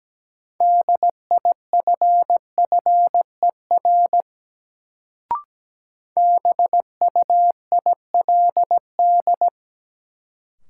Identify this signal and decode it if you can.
Morse code, heard as a keyed tone, about 17 wpm, 700 Hz. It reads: DIFFER BUILD
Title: Sentences from Top 1000 Words - Review Part 1 - New Word Drill - 17wpm